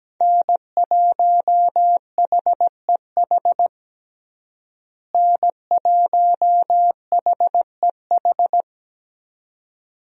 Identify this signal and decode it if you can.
Morse code, heard as a keyed tone, about 17 wpm, 700 Hz. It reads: N1HEH N1HEH
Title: USA Call Signs - 5 Characters - Repeated 2x - 17wpm